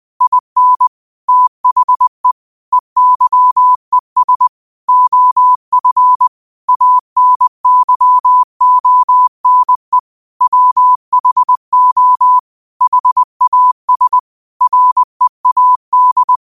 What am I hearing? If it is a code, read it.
Morse code, heard as a keyed tone, about 20 wpm, 1000 Hz. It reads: IN THE EYES OF ANYONE WHO HAS READ
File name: qotdfiles / 20wpm